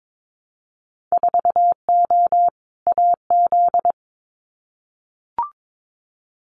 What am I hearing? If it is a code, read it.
Morse code, heard as a keyed tone, about 22 wpm, 700 Hz. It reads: 4O AZ
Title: Winter Field Day Exchanges Encore 22wpm